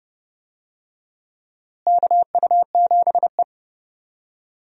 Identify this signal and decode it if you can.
Morse code, heard as a keyed tone, about 30 wpm, 700 Hz. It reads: KU7E
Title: USA Call Signs - 4 Characters - Repeated 2x - 30wpm